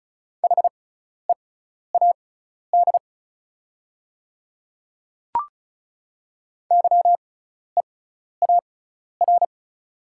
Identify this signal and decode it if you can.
Morse code, heard as a keyed tone, about 34 wpm, spaced as for 12 wpm, 700 Hz. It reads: HEAD YEAR
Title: 6x Character Spacing - Top 500 Words - 35wpm